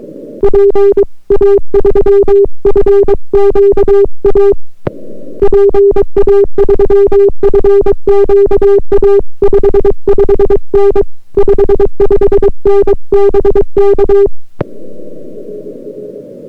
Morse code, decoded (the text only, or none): PA3FQA PA3FQA 55N 55N BK